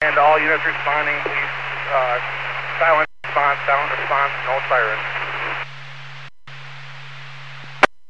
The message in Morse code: none

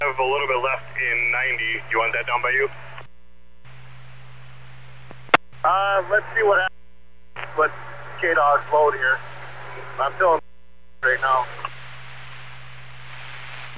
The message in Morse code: none

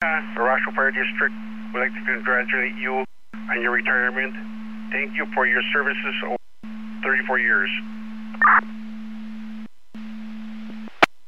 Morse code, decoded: none

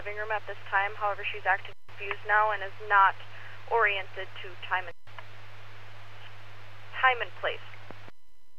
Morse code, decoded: none

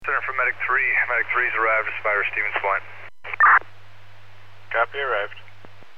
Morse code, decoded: none

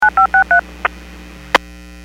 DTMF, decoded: 9563

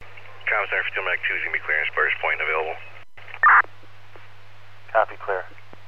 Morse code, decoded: none

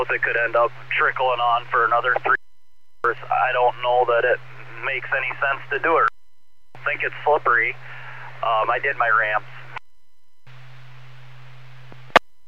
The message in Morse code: none

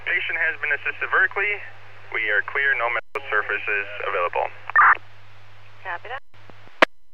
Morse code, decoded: none

